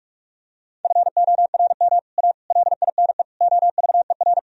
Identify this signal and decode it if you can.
Morse code, heard as a keyed tone, about 47 wpm, 700 Hz. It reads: FORM A LINE OVER
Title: Sentences from Top 1000 Words - Review Part 1 - 45wpm